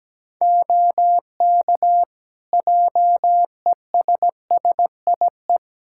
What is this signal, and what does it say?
Morse code, heard as a keyed tone, about 17 wpm, 700 Hz. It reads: OK JESSIE
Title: Practice QSO - Exchange 4 - Ending - 17wpm